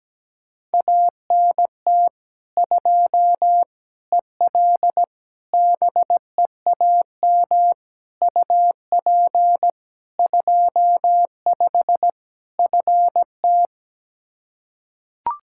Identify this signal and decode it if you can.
Morse code, heard as a keyed tone, about 17 wpm, 700 Hz. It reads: ANT 2 EL BEAM UP 25 FT
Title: Practice QSO - Exchange 2 - Info - 17wpm